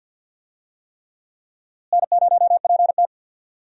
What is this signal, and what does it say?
Morse code, heard as a keyed tone, about 49 wpm, 700 Hz. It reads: N0PT
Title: USA Call Signs - 4 Characters - Repeated 3x - 50wpm